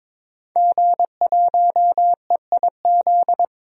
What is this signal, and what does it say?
Morse code, heard as a keyed tone, about 21 wpm, 700 Hz. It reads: G1EIZ
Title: DXCC-100 Call Signs - 5 Characters - 22wpm